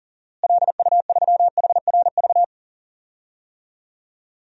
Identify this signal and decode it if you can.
Morse code, heard as a keyed tone, about 40 wpm, 700 Hz. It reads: LU3HRV
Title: DXCC-100 Call Signs - 6 Characters - Repeated 3x - 40wpm